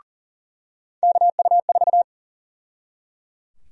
Morse code, heard as a keyed tone, about 40 wpm, 700 Hz. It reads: KU4
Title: USA Call Sign Prefixes - 40wpm